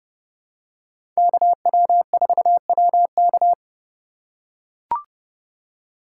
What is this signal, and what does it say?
Morse code, heard as a keyed tone, about 32 wpm, 700 Hz. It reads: KW4WK
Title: USA Call Signs - 5 Characters - 4x Character Spacing - 30wpm